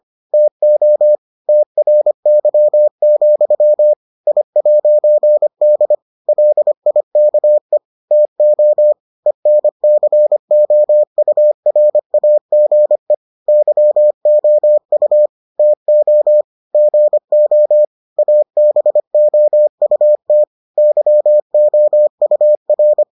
Morse code, heard as a keyed tone, about 25 wpm, 600 Hz. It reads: TO TRY, I'D LIKE TO ENCOURAGE YOU TO GO ABOUT YOUR